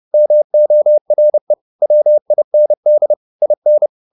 Morse code, heard as a keyed tone, about 30 wpm, 600 Hz. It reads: MORE WIND IN